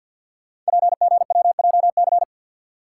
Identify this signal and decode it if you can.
Morse code, heard as a keyed tone, about 50 wpm, 690 Hz. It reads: PGWJC
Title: Code Groups - 5 Letters - 50wpm